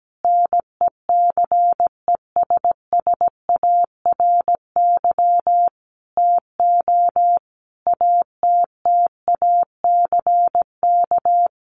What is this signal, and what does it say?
Morse code, heard as a keyed tone, about 17 wpm, 700 Hz. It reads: NECESSARY TO ATTACK
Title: Sentences from Top 1500 Words 17wpm